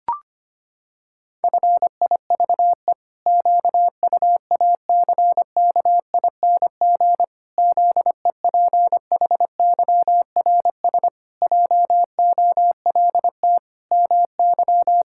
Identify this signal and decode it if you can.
Morse code, heard as a keyed tone, about 25 wpm, 700 Hz. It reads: FIVE QUACKING ZEPHYRS JOLT MY
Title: Pangram Challenge v2 - 25wpm